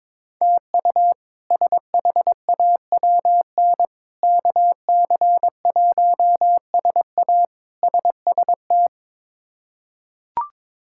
Morse code, heard as a keyed tone, about 23 wpm, 700 Hz. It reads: TU SHAWN KC1SA SST
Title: K1USN SST QSO Practice - 22wpm